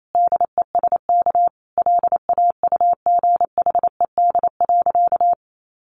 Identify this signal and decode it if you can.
Morse code, heard as a keyed tone, about 28 wpm, 700 Hz. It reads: DESK LAUGHED.
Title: A Sound Of Thunder 28wpm